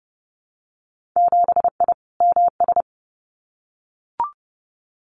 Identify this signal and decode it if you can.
Morse code, heard as a keyed tone, about 30 wpm, 700 Hz. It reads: 7I MS
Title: Winter Field Day Exchanges Encore 30wpm